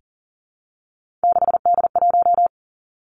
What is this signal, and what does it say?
Morse code, heard as a keyed tone, about 38 wpm, 700 Hz. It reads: 6D1
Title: Three Challenging Letter-Number Combinations 40wpm